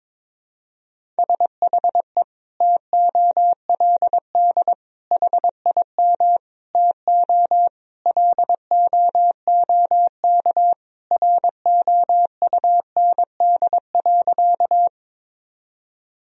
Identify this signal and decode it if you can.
Morse code, heard as a keyed tone, about 21 wpm, 700 Hz. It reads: SHE TOLD HIM TO LOOK ROUND.
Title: Sentences from Top 1000 Words - Review Part 1 - 22wpm